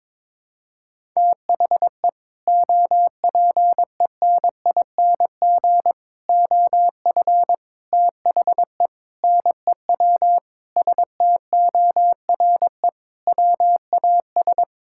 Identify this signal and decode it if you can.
Morse code, heard as a keyed tone, about 22 wpm, 700 Hz. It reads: THE OPENING OF THE NEW STORE WAS
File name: practice / 22wpm / Sentences with 3-Word Phrases - 22wpm